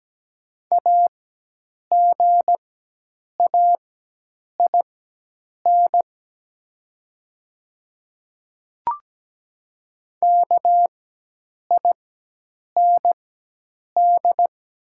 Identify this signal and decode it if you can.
Morse code, heard as a keyed tone, about 17 wpm, spaced as for 8 wpm, 700 Hz. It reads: AGAIN KIND
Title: 4x Character Spacing - Top 200 Words - 17wpm